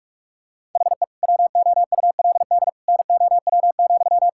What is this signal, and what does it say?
Morse code, heard as a keyed tone, about 43 wpm, 700 Hz. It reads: HE WOULD NOW,
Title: Sentences from Top 1000 Words - Review Part 5 - 45wpm